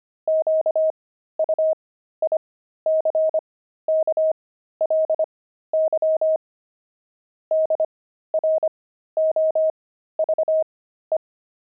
Morse code, heard as a keyed tone, about 25 wpm, spaced as for 13 wpm, 630 Hz. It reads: QUICKLY DROVE
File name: pangram-fs-13-cs-25-pavement